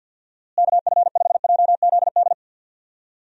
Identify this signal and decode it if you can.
Morse code, heard as a keyed tone, about 48 wpm, 700 Hz. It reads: KF5JZD
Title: USA Call Signs - 6 Characters - Repeated 2x - 50wpm